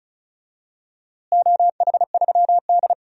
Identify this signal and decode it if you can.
Morse code, heard as a keyed tone, about 34 wpm, 700 Hz. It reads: OH3D
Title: DXCC-100 Call Signs - 4 Characters - Repeated 2x - 35wpm